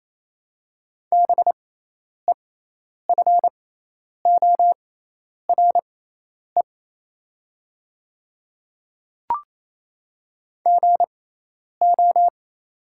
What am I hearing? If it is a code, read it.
Morse code, heard as a keyed tone, about 27 wpm, spaced as for 10 wpm, 700 Hz. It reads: BEFORE GO
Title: 6x Character Spacing - Top 200 Words - 28wpm